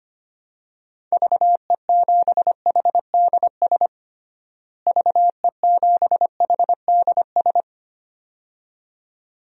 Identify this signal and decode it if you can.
Morse code, heard as a keyed tone, about 25 wpm, 700 Hz. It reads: VE7HDS VE7HDS
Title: DXCC-100 Call Signs - 6 Characters - Repeated 2x - 25wpm